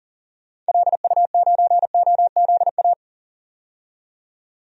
Morse code, heard as a keyed tone, about 40 wpm, 700 Hz. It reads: LU9OZA
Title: DXCC-100 Call Signs - 6 Characters - 40wpm